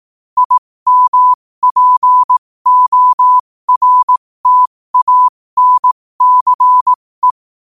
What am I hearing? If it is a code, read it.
Morse code, heard as a keyed tone, about 18 wpm, spaced as for 15 wpm, 1000 Hz. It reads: IMPORTANCE
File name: qotdfiles / 15wpm